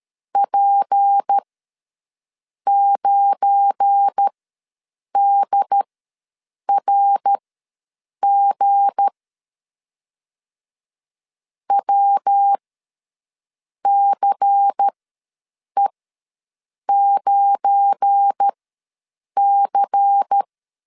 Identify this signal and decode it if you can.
Morse code, heard as a keyed tone, about 13 wpm, spaced as for 6 wpm, 800 Hz. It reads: P9DRG WCE9C
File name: LEZIONI / 7 GF9Z / 7 GF9 ws 12wpm 3344